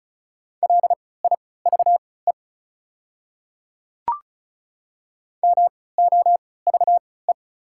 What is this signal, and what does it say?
Morse code, heard as a keyed tone, about 34 wpm, spaced as for 20 wpm, 700 Hz. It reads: LIVE MOVE
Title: 3x Character Spacing - Top 500 Words - 35wpm